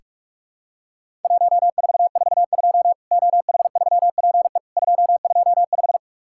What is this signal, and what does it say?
Morse code, heard as a keyed tone, about 44 wpm, 700 Hz. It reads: 1442 OH3PE 225
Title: Worked All Europe DX Contest - QTC Exchange - 45wpm